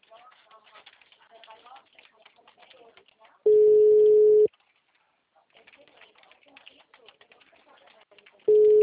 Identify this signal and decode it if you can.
Ringback tone (Europe, 425 Hz), with speech.
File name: out- -2002- - - .38722